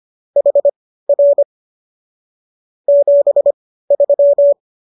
Morse code, heard as a keyed tone, about 25 wpm, spaced as for 15 wpm, 570 Hz. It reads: HR 73